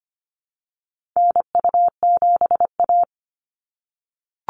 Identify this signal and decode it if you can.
Morse code, heard as a keyed tone, about 25 wpm, 700 Hz. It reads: NU7A
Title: NA SOTA Activator Callsigns 25wpm